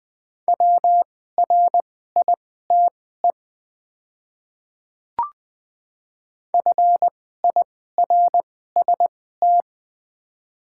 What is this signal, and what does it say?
Morse code, heard as a keyed tone, about 20 wpm, spaced as for 15 wpm, 700 Hz. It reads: WRITE FIRST
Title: 2x Character Spacing - Top 100 Words - 20wpm